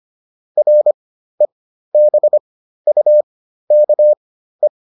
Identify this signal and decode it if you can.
Morse code, heard as a keyed tone, about 25 wpm, spaced as for 13 wpm, 600 Hz. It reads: REBUKE